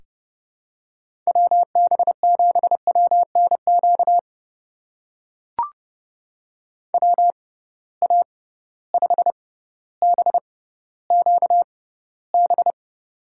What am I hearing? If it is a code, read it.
Morse code, heard as a keyed tone, about 30 wpm, spaced as for 15 wpm, 700 Hz. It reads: WB7WNQ WA5BQB
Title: USA Call Signs - 6 Characters - 6x Character Spacing - 30wpm